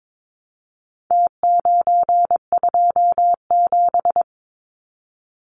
Morse code, heard as a keyed tone, about 22 wpm, 700 Hz. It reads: T927
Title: Numbers - Contest Style - 4-Digit 22wpm